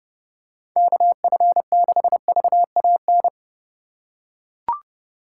Morse code, heard as a keyed tone, about 30 wpm, 700 Hz. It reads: KF6VAN
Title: USA Call Signs - 6 Characters - 4x Character Spacing - 30wpm